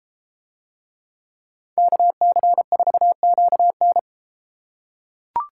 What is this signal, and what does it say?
Morse code, heard as a keyed tone, about 31 wpm, 700 Hz. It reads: KC4QN